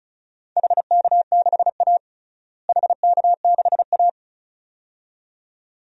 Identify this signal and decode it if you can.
Morse code, heard as a keyed tone, about 36 wpm, 700 Hz. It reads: HK6A HK6A
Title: DXCC-100 Call Signs - 4 Characters - Repeated 2x - 35wpm